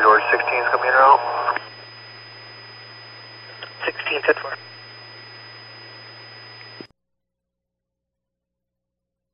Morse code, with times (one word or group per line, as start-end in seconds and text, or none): none